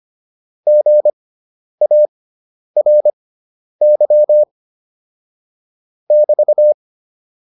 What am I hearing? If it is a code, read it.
Morse code, heard as a keyed tone, about 25 wpm, spaced as for 10 wpm, 600 Hz. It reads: GARY <BT>